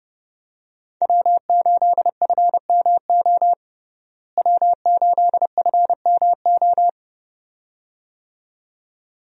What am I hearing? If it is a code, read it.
Morse code, heard as a keyed tone, about 30 wpm, 700 Hz. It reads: W8FMO W8FMO
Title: USA Call Signs - 5 Characters - Repeated 2x - 30wpm